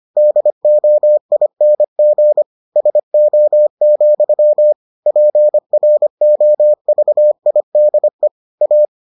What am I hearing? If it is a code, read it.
Morse code, heard as a keyed tone, about 25 wpm, 600 Hz. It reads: DOING SO, PROVIDE A